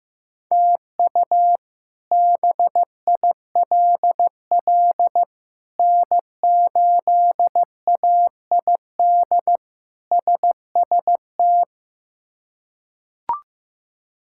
Morse code, heard as a keyed tone, about 15 wpm, 700 Hz. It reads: TU BILL N8AID SST
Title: K1USN SST QSO Practice - 15wpm